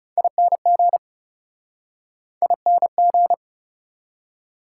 Morse code, heard as a keyed tone, about 37 wpm, 700 Hz. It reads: ING ING
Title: Speed-Racing Prefixes and Suffixes 30wpm